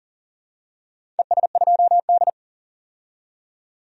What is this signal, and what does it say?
Morse code, heard as a keyed tone, about 40 wpm, 700 Hz. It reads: ES2D
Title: DXCC-100 Call Signs - 4 Characters - 40wpm